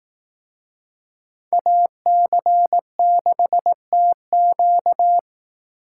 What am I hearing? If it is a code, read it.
Morse code, heard as a keyed tone, about 18 wpm, 700 Hz. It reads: AC6TQ
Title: Speed Racing - USA Call Signs 15wpm